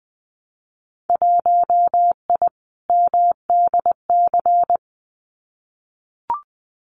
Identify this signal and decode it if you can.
Morse code, heard as a keyed tone, about 20 wpm, 700 Hz. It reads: 1I MDC
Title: Winter Field Day Exchanges 20wpm